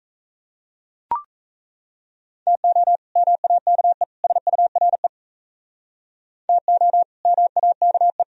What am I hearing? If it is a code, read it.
Morse code, heard as a keyed tone, about 35 wpm, 700 Hz. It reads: TO MAKE SURE TO MAKE